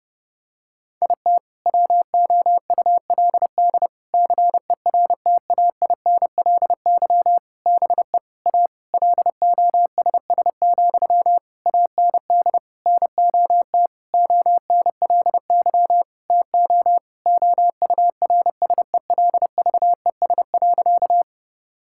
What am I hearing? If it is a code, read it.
Morse code, heard as a keyed tone, about 30 wpm, 700 Hz. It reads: IT WOULD CERTAINLY BE A LOSS, AND NOT ONLY TO OURSELVES.
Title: Sentences from Top 1600 Words 30wpm